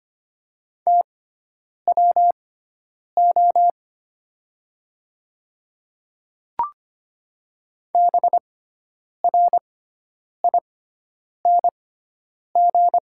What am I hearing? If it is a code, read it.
Morse code, heard as a keyed tone, about 25 wpm, spaced as for 9 wpm, 700 Hz. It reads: TWO BRING
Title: 6x Character Spacing - Top 500 Words - 25wpm